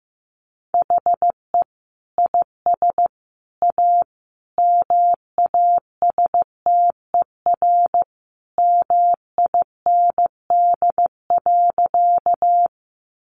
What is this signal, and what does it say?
Morse code, heard as a keyed tone, about 15 wpm, 700 Hz. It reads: HE IS A MASTER MIND.
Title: Sentences from Top 1300 Words 15wpm